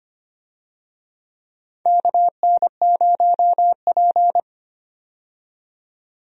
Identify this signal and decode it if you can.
Morse code, heard as a keyed tone, about 25 wpm, 700 Hz. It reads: KN0P
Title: USA Call Signs - 4 Characters - 25wpm